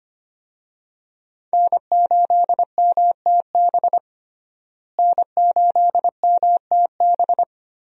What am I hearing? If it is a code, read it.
Morse code, heard as a keyed tone, about 25 wpm, 700 Hz. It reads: N8MTB N8MTB
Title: USA Call Signs - 5 Characters - Repeated 3x - 25wpm